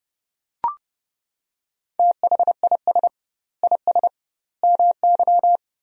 Morse code, heard as a keyed tone, about 30 wpm, 700 Hz. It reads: THIS IS MY
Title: 2x Word-Spacing - Sentences from Top 200 Words - 30wpm